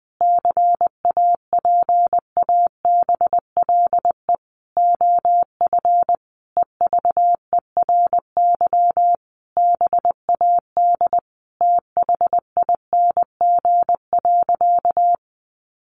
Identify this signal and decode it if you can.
Morse code, heard as a keyed tone, about 20 wpm, 700 Hz. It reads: CAPABLE OF EVERY BAD THING.
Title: Sentences from Top 1200 Words 20wpm